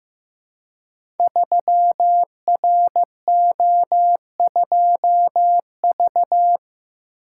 Morse code, heard as a keyed tone, about 15 wpm, 700 Hz. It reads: 3RO2V
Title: Code Groups - 5 Letter:Numbers - 15wpm